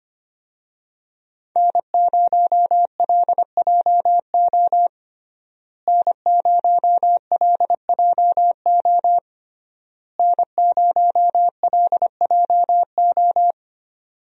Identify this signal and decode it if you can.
Morse code, heard as a keyed tone, about 25 wpm, 700 Hz. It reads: N0LJO N0LJO N0LJO
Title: USA Call Signs - 5 Characters - Repeated 3x - 25wpm